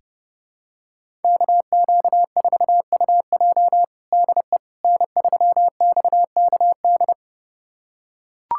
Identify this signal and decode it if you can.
Morse code, heard as a keyed tone, about 32 wpm, 700 Hz. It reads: KQ4UJ DE N3XKD
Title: USA Callsign DE Callsign - 5 Characters - 30wpm